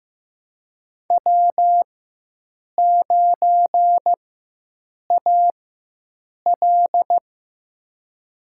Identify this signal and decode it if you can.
Morse code, heard as a keyed tone, about 15 wpm, spaced as for 7 wpm, 700 Hz. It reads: W9AL
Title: USA Call Signs - 4 Characters - 4x Character Spacing - 15wpm